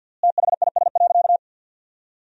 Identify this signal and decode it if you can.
Morse code, heard as a keyed tone, about 50 wpm, 700 Hz. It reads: THIS.
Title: 3x Word-Spacing - Sentences from Top 300 Words - 50wpm